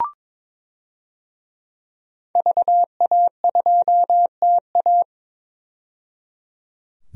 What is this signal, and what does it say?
Morse code, heard as a keyed tone, about 22 wpm, 700 Hz. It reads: VA2TA
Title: DXCC-100 Call Signs - 5 Characters - 22wpm